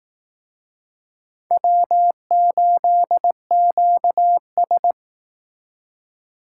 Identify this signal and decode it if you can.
Morse code, heard as a keyed tone, about 18 wpm, 700 Hz. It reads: W8QS
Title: Speed Racing - USA 1x2 Call Signs 15wpm